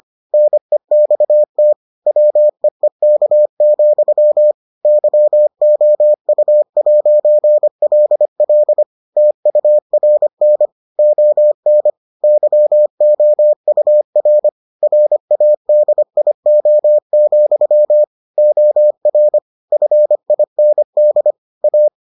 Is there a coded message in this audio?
Yes: Morse code from 0.3 to 21.9 s, heard as a keyed tone, about 25 wpm, 600 Hz. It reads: NEXT WEEK, YOU'LL TURN ON YOUR RADIO, OR FIND A